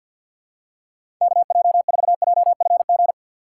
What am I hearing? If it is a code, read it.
Morse code, heard as a keyed tone, about 53 wpm, 700 Hz. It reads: KJ4JFG
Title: USA Call Signs - 6 Characters - 6x Character Spacing - 50wpm